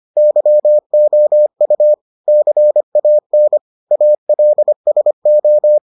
Morse code, heard as a keyed tone, about 25 wpm, 600 Hz. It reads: YOU CAN ALSO